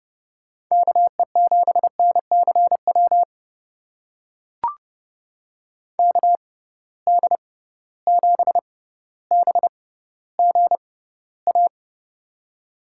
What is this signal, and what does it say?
Morse code, heard as a keyed tone, about 30 wpm, spaced as for 15 wpm, 700 Hz. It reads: KE7NCW KD7BGA
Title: USA Call Signs - 6 Characters - 6x Character Spacing - 30wpm